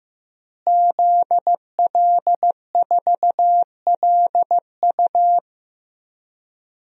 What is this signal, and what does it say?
Morse code, heard as a keyed tone, about 16 wpm, 700 Hz. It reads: ZL4LU
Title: DXCC-100 Call Signs - 5 Characters - Repeated 3x - 15wpm